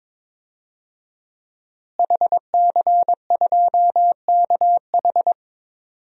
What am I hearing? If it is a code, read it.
Morse code, heard as a keyed tone, about 22 wpm, 700 Hz. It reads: HC2KH
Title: DXCC-100 Call Signs - 5 Characters - Repeated 2x - 22wpm